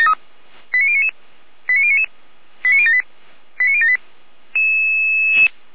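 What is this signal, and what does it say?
Five-tone selective call (ZVEI), heard as quick runs of five tones, tones of about 70 ms. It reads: 89099 89099 89078 89078
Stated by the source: ZVEI five-tone selective call received by SDR.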